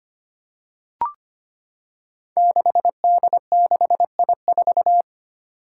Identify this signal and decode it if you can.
Morse code, heard as a keyed tone, about 25 wpm, 700 Hz. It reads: 6D6I4
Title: Five Challenging Letter-Number Combinations 25wpm